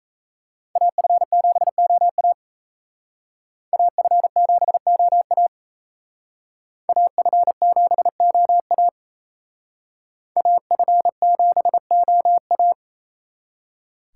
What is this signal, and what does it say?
Morse code, heard as a keyed tone, about 35 wpm, 700 Hz. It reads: AF7OA AF7OA AF7OA AF7OA
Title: Speed Racing - USA Call Signs 28wpm